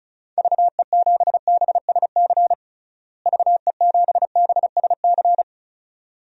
Morse code, heard as a keyed tone, about 34 wpm, 700 Hz. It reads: VE7BSC VE7BSC
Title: DXCC-100 Call Signs - 6 Characters - Repeated 2x - 35wpm